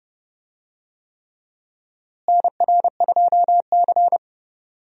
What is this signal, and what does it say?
Morse code, heard as a keyed tone, about 32 wpm, 700 Hz. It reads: NR2C
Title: USA Call Signs - 4 Characters - Repeated 3x - 30wpm